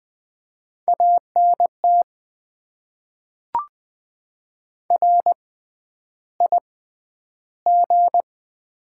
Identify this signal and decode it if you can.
Morse code, heard as a keyed tone, about 20 wpm, spaced as for 10 wpm, 700 Hz. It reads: ANT RIG
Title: Top 20 QSO Elements - 6x Character Spacing - 20wpm